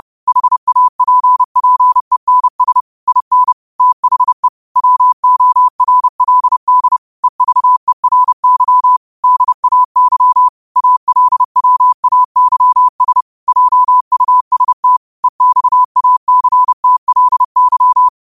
Morse code, heard as a keyed tone, about 30 wpm, 1000 Hz. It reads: HAPPENS IN THE WORLD EVERY DAY ALWAYS JUST EXACTLY